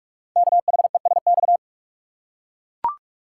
Morse code, heard as a keyed tone, about 44 wpm, 700 Hz. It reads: KHESX
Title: Code Groups - 5 Letters - 45wpm